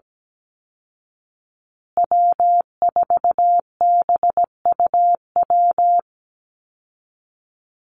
Morse code, heard as a keyed tone, about 17 wpm, 700 Hz. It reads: W4BUW
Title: Call Signs Repeated 2x - 17wpm